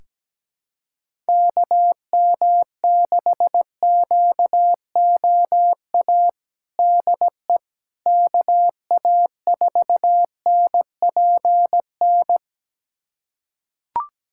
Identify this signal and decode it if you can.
Morse code, heard as a keyed tone, about 17 wpm, 700 Hz. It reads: KM6QOA DE KA4NPN
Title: USA Callsign DE Callsign - 6 Characters - 17wpm